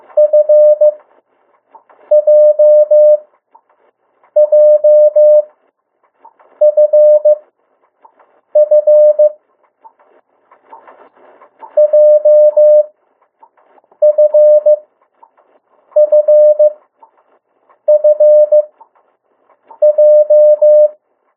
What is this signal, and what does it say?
Morse code, heard as a keyed tone, about 15 wpm, spaced as for 6 wpm, 600 Hz. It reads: FJJFF JFFFJ
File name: urok 1